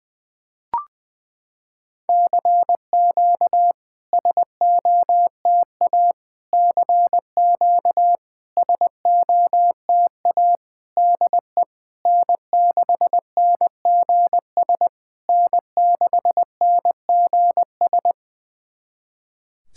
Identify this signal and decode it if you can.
Morse code, heard as a keyed tone, about 21 wpm, 700 Hz. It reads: CQ SOTA CQ SOTA DE N6NGS N6NGS
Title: SOTA QSO Practice 20wpm